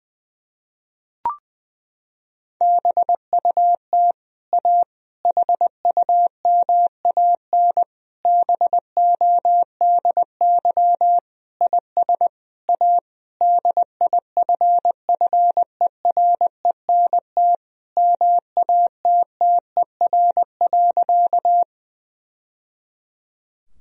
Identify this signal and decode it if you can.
Morse code, heard as a keyed tone, about 21 wpm, 700 Hz. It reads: BUT A HUMAN BODY IS A DIFFERENT MATTER.
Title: Sentences from Top 1000 Words - Review Part 3 - 20wpm